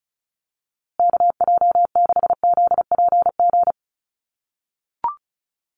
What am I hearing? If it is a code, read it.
Morse code, heard as a keyed tone, about 34 wpm, 700 Hz. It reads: KJ6ZPG
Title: USA Call Signs Repeated 3x 35wpm